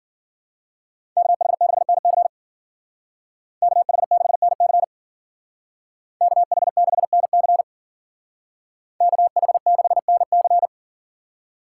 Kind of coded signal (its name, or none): Morse code